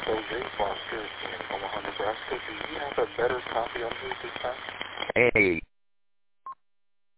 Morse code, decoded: none

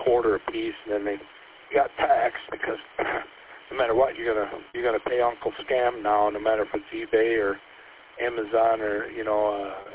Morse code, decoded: none

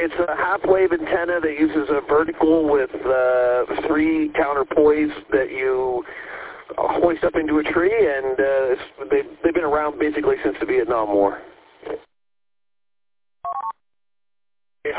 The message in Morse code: none

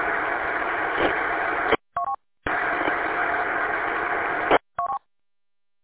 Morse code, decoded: none

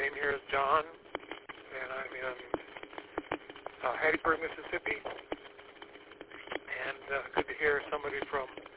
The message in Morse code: none

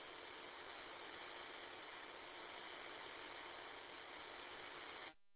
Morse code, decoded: none

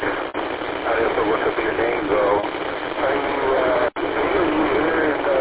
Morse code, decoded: none

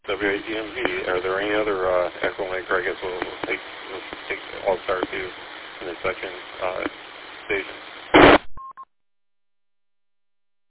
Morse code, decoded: none

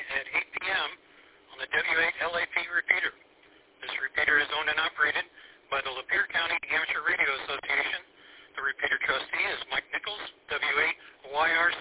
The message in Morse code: none

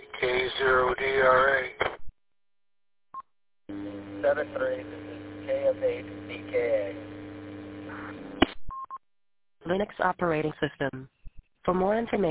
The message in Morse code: none

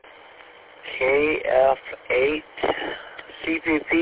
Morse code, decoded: none

